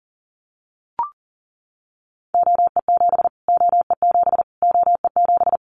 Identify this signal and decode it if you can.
Morse code, heard as a keyed tone, about 40 wpm, 700 Hz. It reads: OE7 OE7 OE7
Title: Callsign Prefixes for Top 25 DXCC - Repeated 3x - 40wpm